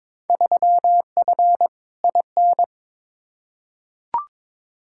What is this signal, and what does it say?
Morse code, heard as a keyed tone, about 22 wpm, 700 Hz. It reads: 3F IN
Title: Field Day Exchanges - 22wpm